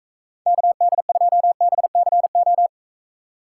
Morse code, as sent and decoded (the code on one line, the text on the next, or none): -.- -.. ..--- -... -.-. ---
KD2BCO